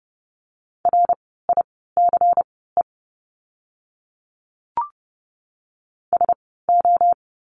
Morse code, heard as a keyed tone, about 30 wpm, spaced as for 17 wpm, 700 Hz. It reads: RICE SO
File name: Lesson 16f - Letter c - Words 30wpm